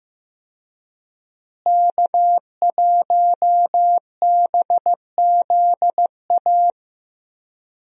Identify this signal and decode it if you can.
Morse code, heard as a keyed tone, about 15 wpm, 700 Hz. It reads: K1BZA
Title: USA Call Signs - 5 Characters - Repeated 3x - 15wpm